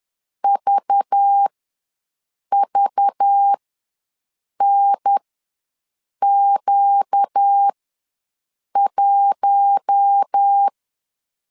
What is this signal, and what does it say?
Morse code, heard as a keyed tone, about 10 wpm, spaced as for 6 wpm, 800 Hz. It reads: VVNQ1
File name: LEZIONI / 5 LBVC / 5 LBV ws 10wpm 3645